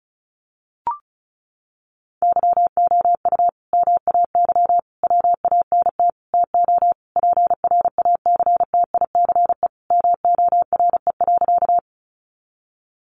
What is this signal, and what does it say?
Morse code, heard as a keyed tone, about 34 wpm, 700 Hz. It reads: YOU MAY WANT TO PRACTICE MORE.
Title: Sentences from Top 700 Words 35wpm